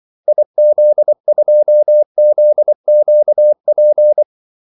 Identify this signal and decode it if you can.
Morse code, heard as a keyed tone, about 24 wpm, 600 Hz. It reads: IZ2ZQP
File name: Callsigns-24wpm